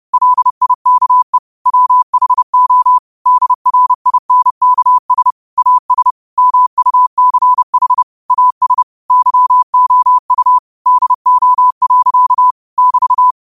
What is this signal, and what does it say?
Morse code, heard as a keyed tone, about 30 wpm, 1000 Hz. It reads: LIKE WHO DRINKS AS MUCH AS YOU DO. <BT>
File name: qotdfiles / 30wpm